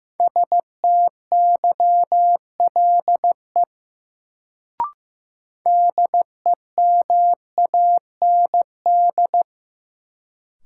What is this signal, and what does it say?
Morse code, heard as a keyed tone, about 15 wpm, 700 Hz. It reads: STYLE DEMAND
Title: Sentences from Top 800 Words - New Word Drill - 15wpm